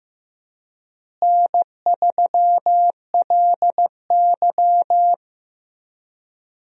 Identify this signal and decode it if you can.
Morse code, heard as a keyed tone, about 15 wpm, 700 Hz. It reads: N3LY
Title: USA Call Signs - 4 Characters - Repeated 2x - 15wpm